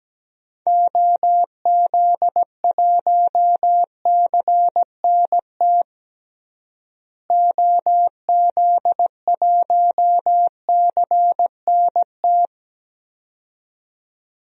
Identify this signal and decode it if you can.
Morse code, heard as a keyed tone, about 17 wpm, 700 Hz. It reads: OZ1CNT OZ1CNT
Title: DXCC-100 Call Signs - 6 Characters - Repeated 2x - 17wpm